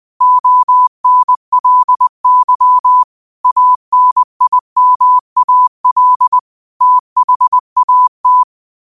Morse code, heard as a keyed tone, about 20 wpm, 1000 Hz. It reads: ONLY ANIMAL THAT
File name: qotdfiles / 20wpm